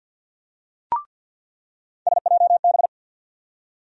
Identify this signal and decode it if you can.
Morse code, heard as a keyed tone, about 46 wpm, 700 Hz. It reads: SJB
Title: USA Call Sign Postfixes - 3 Letters 50wpm